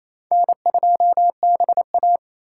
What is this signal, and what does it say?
Morse code, heard as a keyed tone, about 27 wpm, 700 Hz. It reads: N2BA
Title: USA Call Signs - 4 Characters - 4x Character Spacing - 28wpm